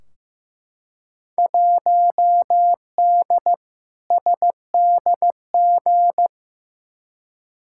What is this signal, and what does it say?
Morse code, heard as a keyed tone, about 15 wpm, 700 Hz. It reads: 1D SDG
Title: Field Day Exchanges - 15wpm